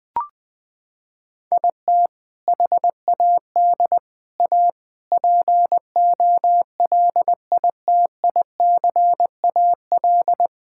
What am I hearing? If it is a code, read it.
Morse code, heard as a keyed tone, about 20 wpm, 700 Hz. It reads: IT HAD A POLITICAL
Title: Sentences from Top 1000 Words - Review Part 5 - 20wpm